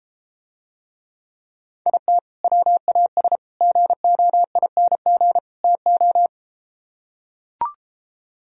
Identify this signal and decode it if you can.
Morse code, heard as a keyed tone, about 33 wpm, 700 Hz. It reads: IT WAS GOING TO